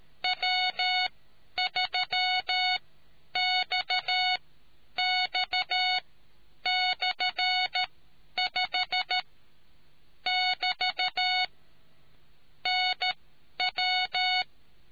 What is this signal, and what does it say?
Morse code, heard as a keyed tone, about 13 wpm, spaced as for 9 wpm, 740 Hz. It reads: W3XX/5 <BT> NW